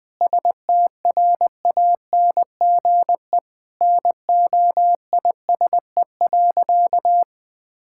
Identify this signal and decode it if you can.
Morse code, heard as a keyed tone, about 20 wpm, 700 Hz. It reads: STRANGE NOISE.
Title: Sentences with 8-Word Phrases - 20wpm